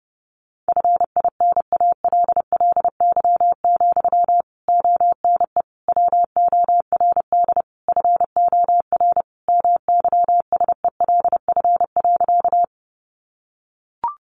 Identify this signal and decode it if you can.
Morse code, heard as a keyed tone, about 28 wpm, 700 Hz. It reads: FINALLY, ONE WORD FOR MYSELF.
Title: Sentences from Top 1300 Words 30wpm